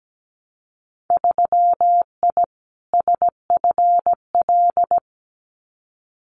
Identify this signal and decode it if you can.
Morse code, heard as a keyed tone, about 17 wpm, 700 Hz. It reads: 3I SFL
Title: Winter Field Day Exchanges 17wpm